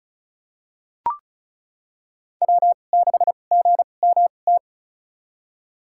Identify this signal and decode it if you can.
Morse code, heard as a keyed tone, about 36 wpm, spaced as for 25 wpm, 700 Hz. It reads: W6GMT
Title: USA Call Signs - 2x Character Spacing - 35wpm